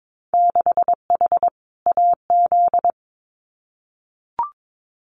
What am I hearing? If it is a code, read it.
Morse code, heard as a keyed tone, about 22 wpm, 700 Hz. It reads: 6H AZ
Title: Winter Field Day Exchanges Encore 22wpm